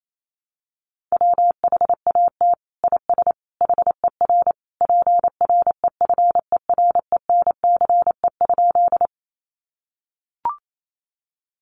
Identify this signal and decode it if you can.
Morse code, heard as a keyed tone, about 28 wpm, 700 Hz. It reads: WHAT IS HER PREFERENCE?
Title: Sentences from Top 1500 Words 28wpm